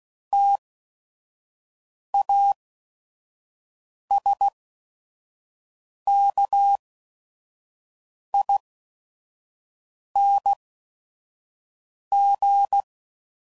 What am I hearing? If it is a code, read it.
Morse code, heard as a keyed tone, about 16 wpm, spaced as for 5 wpm, 800 Hz. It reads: TASKING
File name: Suffixes / Special ING's / VerySlowToFaster ING's